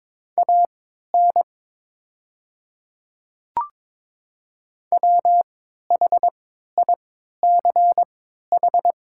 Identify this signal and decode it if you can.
Morse code, heard as a keyed tone, about 22 wpm, spaced as for 12 wpm, 700 Hz. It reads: AN WHICH
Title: 3x Character Spacing - Top 100 Words - 22wpm